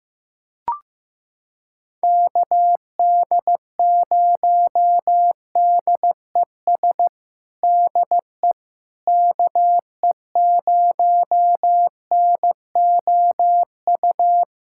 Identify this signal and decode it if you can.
Morse code, heard as a keyed tone, about 16 wpm, 700 Hz. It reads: KD0DES DE KE0NOU
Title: USA Callsign DE Callsign - 6 Characters - 15wpm